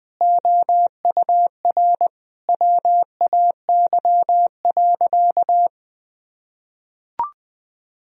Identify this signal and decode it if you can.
Morse code, heard as a keyed tone, about 20 wpm, 700 Hz. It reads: OUR WAY.
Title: Sentences from Top 1000 Words - Review Part 1 - 20wpm